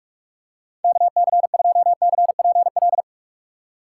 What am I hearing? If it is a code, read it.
Morse code, heard as a keyed tone, about 46 wpm, 700 Hz. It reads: KC2CPL
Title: USA Call Signs - 4x Character Spacing - 45wpm